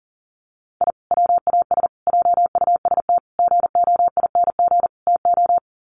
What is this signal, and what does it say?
Morse code, heard as a keyed tone, about 40 wpm, 700 Hz. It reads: I WAS JUST GOING TO
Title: Sets of 5 Words Encore 40wpm